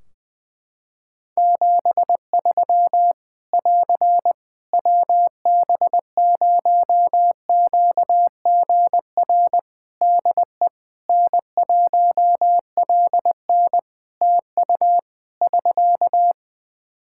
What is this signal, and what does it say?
Morse code, heard as a keyed tone, about 19 wpm, 700 Hz. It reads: 73 <AR> WB0QGR DE N1LN TU <SK>
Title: Practice QSO - Exchange 4 - Ending - 20wpm